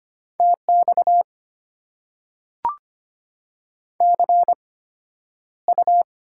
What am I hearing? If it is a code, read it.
Morse code, heard as a keyed tone, about 25 wpm, spaced as for 11 wpm, 700 Hz. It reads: TX CU
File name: Top 90 QSO Elements - 8x Character Spacing - 25wpm